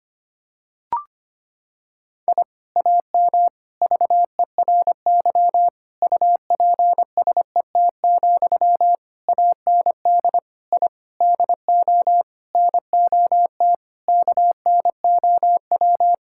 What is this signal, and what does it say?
Morse code, heard as a keyed tone, about 25 wpm, 700 Hz. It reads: I AM VERY UPSET, AND I DO NOT KNOW
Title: Sentences from Top 1000 Words - Review Part 5 - 25wpm